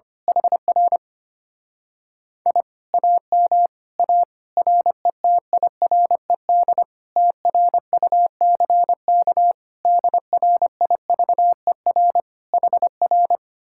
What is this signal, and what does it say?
Morse code, heard as a keyed tone, about 24 wpm, 700 Hz. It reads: HR I AM A RETIRED TRUCK DRIVER HR
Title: Speed-Racing Common QSO Phrases - 25wpm